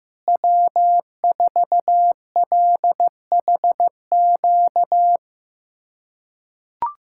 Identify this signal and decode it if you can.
Morse code, heard as a keyed tone, about 15 wpm, 700 Hz. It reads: W4LHQ
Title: USA Callsign DE Callsign - 5 Characters - 15wpm